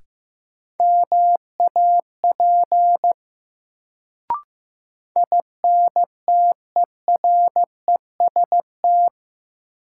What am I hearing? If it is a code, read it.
Morse code, heard as a keyed tone, about 15 wpm, 700 Hz. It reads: MAP INTEREST
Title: Sentences from Top 400 Words - New Word Drill - 15wpm